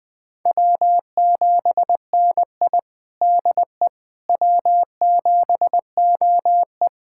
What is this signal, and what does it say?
Morse code, heard as a keyed tone, about 20 wpm, 700 Hz. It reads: W7NI DE W7OE
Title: USA Callsign DE Callsign - 4 Characters - 20wpm